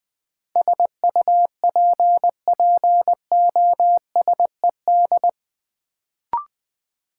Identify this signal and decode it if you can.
Morse code, heard as a keyed tone, about 20 wpm, 700 Hz. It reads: SUPPOSED
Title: Sets of 3 Words - New Word Drill - 20wpm